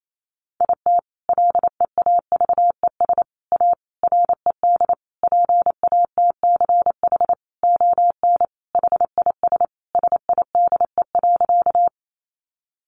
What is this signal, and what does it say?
Morse code, heard as a keyed tone, about 28 wpm, 700 Hz. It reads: IT LEAVES A RED PATCH ON HIS SIDE.
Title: A Sound Of Thunder 28wpm